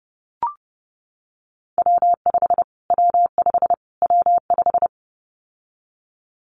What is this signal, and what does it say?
Morse code, heard as a keyed tone, about 30 wpm, 700 Hz. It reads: W5 W5 W5
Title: Callsign Prefixes for Top 25 DXCC - Repeated 3x - 30wpm